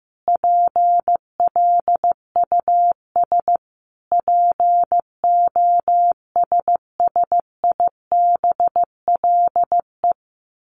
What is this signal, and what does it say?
Morse code, heard as a keyed tone, about 15 wpm, 700 Hz. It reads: PLUS POSSIBLE
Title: A Sound Of Thunder 15wpm